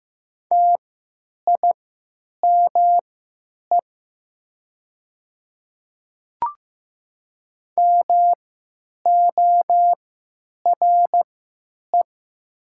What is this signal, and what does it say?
Morse code, heard as a keyed tone, about 15 wpm, spaced as for 9 wpm, 700 Hz. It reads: TIME MORE
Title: 3x Character Spacing - Top 100 Words - 15wpm